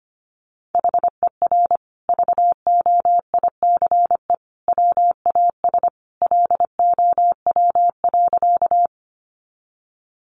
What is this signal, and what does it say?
Morse code, heard as a keyed tone, about 25 wpm, 700 Hz. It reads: HER VOICE WAS LOW.
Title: Sentences from Top 500 Words 25wpm